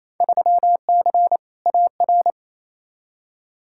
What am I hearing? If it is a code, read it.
Morse code, heard as a keyed tone, about 28 wpm, 700 Hz. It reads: 3C AR
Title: Field Day Exchanges Encore - 28wpm